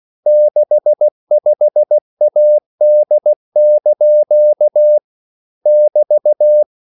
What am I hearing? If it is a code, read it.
Morse code, heard as a keyed tone, about 16 wpm, 600 Hz. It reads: 65AD) <BT>
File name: CW / Practice / EnglishQuotations0216